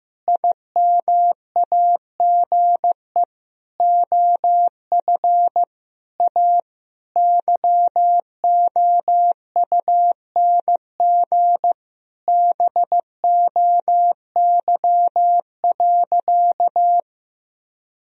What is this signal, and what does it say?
Morse code, heard as a keyed tone, about 15 wpm, 700 Hz. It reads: IMAGE OF A YOUNG BOY.
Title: Sentences from Top 1000 Words - Review Part 4 - 15wpm